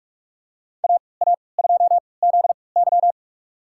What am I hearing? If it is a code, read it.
Morse code, heard as a keyed tone, about 46 wpm, spaced as for 26 wpm, 700 Hz. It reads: AA2ZY
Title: USA Call Signs - 5 Characters - 3x Character Spacing - 45wpm